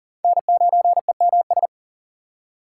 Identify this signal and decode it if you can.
Morse code, heard as a keyed tone, about 37 wpm, 700 Hz. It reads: N9EMS
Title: USA Call Signs - 6x Character Spacing - 40wpm